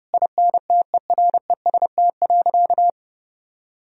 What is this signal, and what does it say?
Morse code, heard as a keyed tone, about 30 wpm, 700 Hz. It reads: INTEREST.
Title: Sentences from Top 1000 Words - Review Part 4 - 30wpm